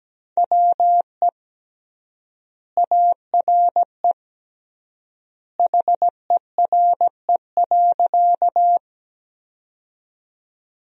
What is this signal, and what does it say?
Morse code, heard as a keyed tone, about 17 wpm, 700 Hz. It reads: WE ARE HERE.
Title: 3x Word-Spacing - Sentences from Top 200 Words - 17wpm